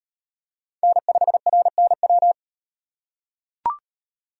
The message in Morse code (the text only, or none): N5RNW